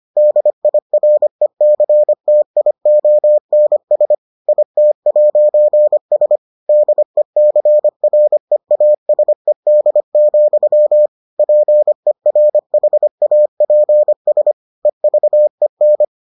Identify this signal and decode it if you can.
Morse code, heard as a keyed tone, about 25 wpm, 600 Hz. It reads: DIRECTIONS IT'S DECREASED, PERHAPS EVEN